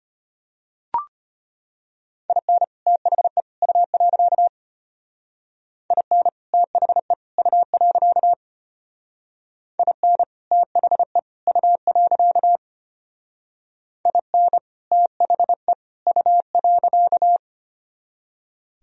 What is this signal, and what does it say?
Morse code, heard as a keyed tone, about 32 wpm, 700 Hz. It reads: IN THE U. IN THE U. IN THE U. IN THE U.